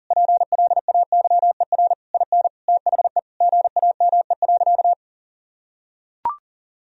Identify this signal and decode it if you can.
Morse code, heard as a keyed tone, about 40 wpm, 700 Hz. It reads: PLAYER IN THE GAME.
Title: Sentences from Top 1000 Words - Review Part 4 - 40wpm